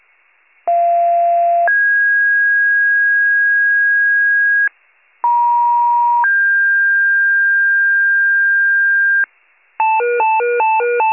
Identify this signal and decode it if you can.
Two-tone page, heard as a one-second tone, then a three-second tone, about 690 then 1650 Hz, about 950 then 1600 Hz. It is heard from 0.7 to 9.2 s.